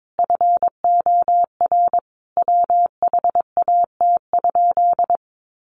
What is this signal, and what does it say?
Morse code, heard as a keyed tone, about 22 wpm, 700 Hz. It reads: FOR WHAT?
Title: Sentences from Top 700 Words 22wpm